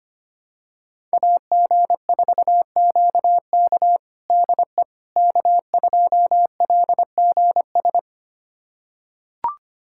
Morse code, heard as a keyed tone, about 25 wpm, 700 Hz. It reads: AG4QK DE K2LGS
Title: USA Callsign DE Callsign - 5 Characters - 25wpm